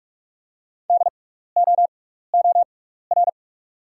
Morse code, heard as a keyed tone, about 44 wpm, spaced as for 15 wpm, 700 Hz. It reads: DOOR
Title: 6x Character Spacing - Top 500 Words - 45wpm